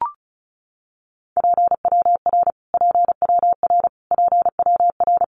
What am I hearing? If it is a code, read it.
Morse code, heard as a keyed tone, about 34 wpm, 700 Hz. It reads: PWR PWR PWR
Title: Abbreviations Repeated 3x - 35wpm